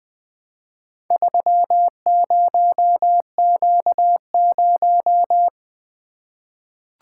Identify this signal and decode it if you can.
Morse code, heard as a keyed tone, about 20 wpm, 700 Hz. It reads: 30Q0
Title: Four Letter-Number Combinations - 20wpm